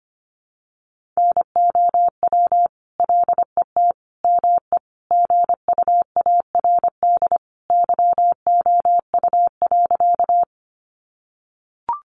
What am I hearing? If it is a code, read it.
Morse code, heard as a keyed tone, about 25 wpm, 700 Hz. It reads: NOW LET ME GUARD YOU.
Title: Sentences from Top 700 Words 25wpm